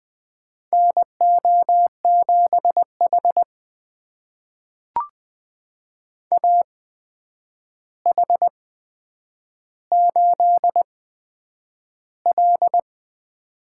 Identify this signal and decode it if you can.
Morse code, heard as a keyed tone, about 20 wpm, spaced as for 9 wpm, 700 Hz. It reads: NO7H AH8L
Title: USA Call Signs - 4 Characters - 8x Character Spacing - 20wpm